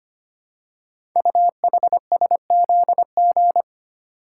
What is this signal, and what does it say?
Morse code, heard as a keyed tone, about 25 wpm, 700 Hz. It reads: UHSZG
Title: Code Groups - 5 Letters - 25wpm